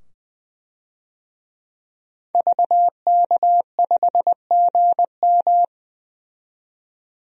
Morse code, heard as a keyed tone, about 20 wpm, 700 Hz. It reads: VK5GM
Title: DXCC-100 Call Signs - 5 Characters - 20wpm